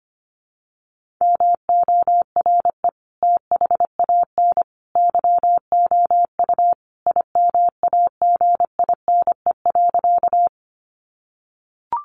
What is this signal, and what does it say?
Morse code, heard as a keyed tone, about 25 wpm, 700 Hz. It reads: MORE THAN YOU IMAGINE.
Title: Sentences from Top 1400 Words 25wpm